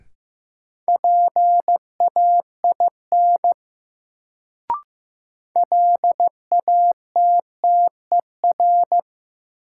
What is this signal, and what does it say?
Morse code, heard as a keyed tone, about 15 wpm, 700 Hz. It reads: PAIN LATTER
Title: Sentences from Top 1600 Words - New Word Drill Alt - 15wpm